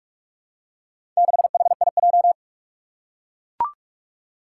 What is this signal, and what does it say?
Morse code, heard as a keyed tone, about 44 wpm, 700 Hz. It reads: 6HIJ
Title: Four Challenging Letter-Number Combinations 45wpm